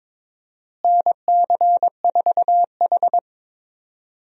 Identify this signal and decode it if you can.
Morse code, heard as a keyed tone, about 22 wpm, 700 Hz. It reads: NC4H
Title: USA Call Signs - 4 Characters - Repeated 3x - 22wpm